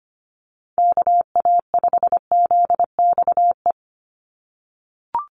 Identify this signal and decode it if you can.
Morse code, heard as a keyed tone, about 25 wpm, 700 Hz. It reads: KA5ZXE
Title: USA Call Signs 25wpm